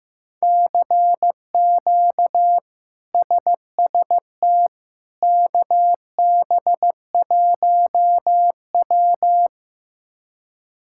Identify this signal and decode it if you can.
Morse code, heard as a keyed tone, about 16 wpm, 700 Hz. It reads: CQ SST KB1W
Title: K1USN SST QSO Practice - 15wpm